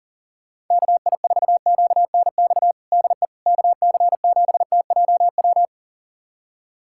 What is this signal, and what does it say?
Morse code, heard as a keyed tone, about 40 wpm, 700 Hz. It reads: KI4QNX DE KC7TJW
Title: USA Callsign DE Callsign - 6 Characters - 40wpm